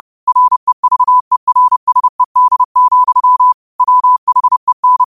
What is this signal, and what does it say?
Morse code, heard as a keyed tone, about 30 wpm, 1000 Hz. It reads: REVERSED, WHEN
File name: qotdfiles / 30wpm